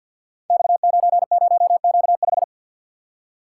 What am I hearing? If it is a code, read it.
Morse code, heard as a keyed tone, about 53 wpm, 700 Hz. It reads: X90Q5
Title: Code Groups - 5 Letter:Numbers - 50wpm